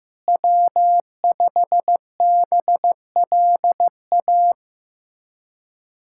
Morse code, heard as a keyed tone, about 15 wpm, 700 Hz. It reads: W5BLA
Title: USA Callsign DE Callsign - 5 Characters - 15wpm